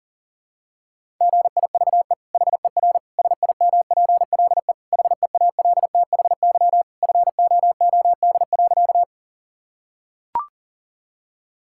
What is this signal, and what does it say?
Morse code, heard as a keyed tone, about 37 wpm, 700 Hz. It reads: GIVE HER SIMPLE HEALTHY FOOD.
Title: Sentences from Top 1000 Words - Review Part 5 - 40wpm